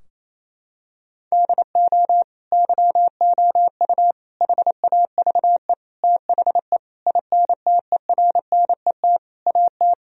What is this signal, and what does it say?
Morse code, heard as a keyed tone, about 28 wpm, 700 Hz. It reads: DO YOU HAVE THE INTERNET AT
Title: Sentences from Top 1000 Words - Review Part 4 - 28wpm